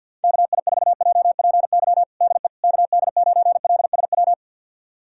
Morse code, heard as a keyed tone, about 47 wpm, 700 Hz. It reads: KI4JPY DE KD9LSW
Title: USA Callsign DE Callsign 50wpm